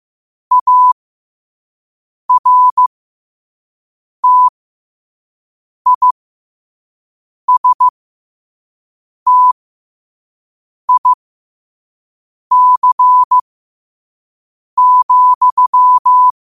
Morse code, heard as a keyed tone, about 15 wpm, spaced as for 5 wpm, 1000 Hz. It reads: ARTISTIC,
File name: qotdfiles / 5wpm